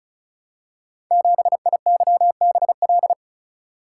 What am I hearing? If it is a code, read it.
Morse code, heard as a keyed tone, about 36 wpm, 700 Hz. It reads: 7IYBL
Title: Code Groups - 5 Letter:Numbers - 35wpm